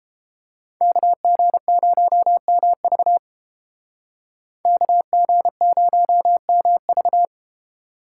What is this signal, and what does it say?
Morse code, heard as a keyed tone, about 31 wpm, 700 Hz. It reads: KG0MV KG0MV